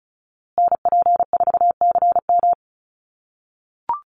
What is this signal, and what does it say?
Morse code, heard as a keyed tone, about 34 wpm, 700 Hz. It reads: NP4CM
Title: USA Call Signs Repeated 3x 35wpm